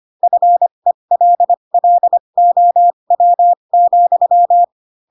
Morse code, heard as a keyed tone, about 25 wpm, spaced as for 22 wpm, 700 Hz. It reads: FELLOW,
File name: Pinocchio / Pinocchio ch3 18 25 ik1hge